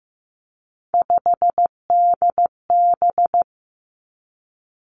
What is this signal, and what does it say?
Morse code, heard as a keyed tone, about 15 wpm, 700 Hz. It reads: 5DB
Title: Three Challenging Letter-Number Combinations 15wpm